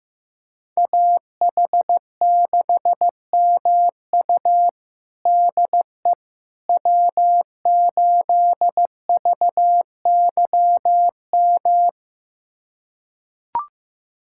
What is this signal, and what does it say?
Morse code, heard as a keyed tone, about 15 wpm, 700 Hz. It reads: AH6MU DE W8VYM
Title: USA Callsign DE Callsign - 5 Characters - 15wpm